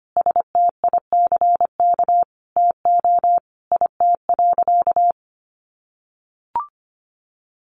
Morse code, heard as a keyed tone, about 25 wpm, 700 Hz. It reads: STICK TO IT.
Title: Sentences from Top 1200 Words 25wpm